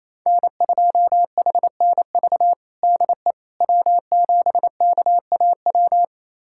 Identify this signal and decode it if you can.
Morse code, heard as a keyed tone, about 28 wpm, 700 Hz. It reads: N2HNV DE W7KAW
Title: USA Callsign DE Callsign - 5 Characters - 28wpm